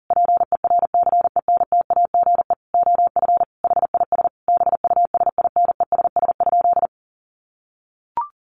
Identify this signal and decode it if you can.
Morse code, heard as a keyed tone, about 40 wpm, 700 Hz. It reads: PERCENTAGE OF HIS BUSINESS?
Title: Sentences from Top 1300 Words 40wpm